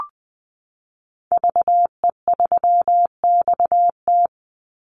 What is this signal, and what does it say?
Morse code, heard as a keyed tone, about 20 wpm, 700 Hz. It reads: VE3XT
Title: Call Signs Repeated 2x - 20wpm